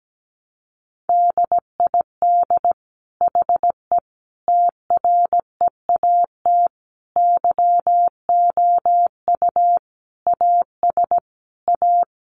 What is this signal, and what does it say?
Morse code, heard as a keyed tone, about 17 wpm, 700 Hz. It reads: DID HE TREAT YOU AS A
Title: Sentences from Top 1300 Words 17wpm